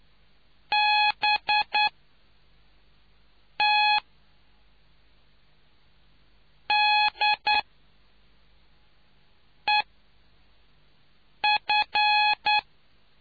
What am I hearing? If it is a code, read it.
Morse code, heard as a keyed tone, about 9 wpm, spaced as for 4 wpm, 840 Hz. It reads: BTDEF